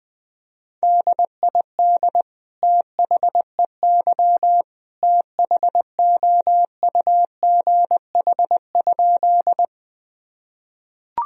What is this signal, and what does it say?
Morse code, heard as a keyed tone, about 20 wpm, 700 Hz. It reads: DID THEY THOUGH?
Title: Sentences from Top 1000 Words - Review Part 2 - 20wpm